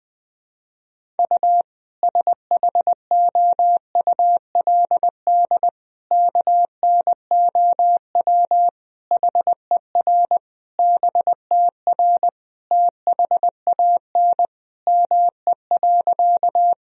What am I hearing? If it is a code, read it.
Morse code, heard as a keyed tone, about 21 wpm, 700 Hz. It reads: U SHOULD KNOW HER BTR THAN ME.
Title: Sentences from Top 400 Words and Abbreviations 20wpm